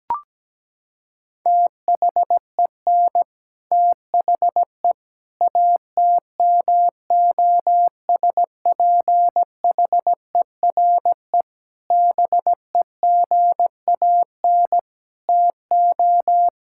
Morse code, heard as a keyed tone, about 17 wpm, 700 Hz. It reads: THEN THE ATMOSPHERE BEGAN TO
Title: Sentences from Top 1600 Words 17wpm